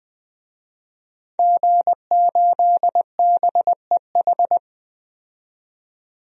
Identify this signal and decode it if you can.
Morse code, heard as a keyed tone, about 20 wpm, 700 Hz. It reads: G8BEH
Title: DXCC-100 Call Signs - 5 Characters - 20wpm